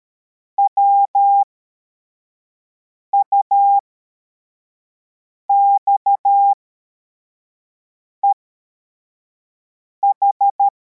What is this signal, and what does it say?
Morse code, heard as a keyed tone, about 13 wpm, spaced as for 4 wpm, 800 Hz. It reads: WUXEH